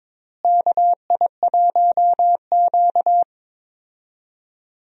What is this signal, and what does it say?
Morse code, heard as a keyed tone, about 22 wpm, 700 Hz. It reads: KI1Q
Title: USA Call Signs - 4 Characters - Repeated 2x - 22wpm